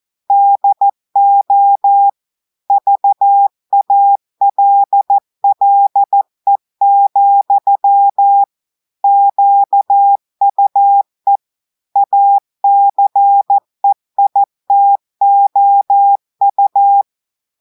Morse code, heard as a keyed tone, about 14 wpm, 800 Hz. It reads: DO VALLE, QUE ACEITOU
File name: audios / portugais / portugais texte en clair 1 vitesse 14 mots minutes